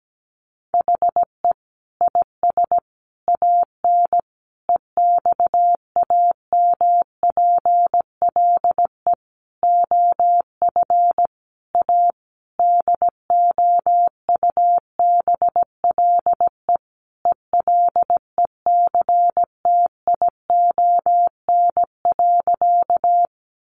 Morse code, heard as a keyed tone, about 17 wpm, 700 Hz. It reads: HE IS AN EXAMPLE OF A DOUBLE ELECTION.
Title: Sentences from Top 1500 Words 17wpm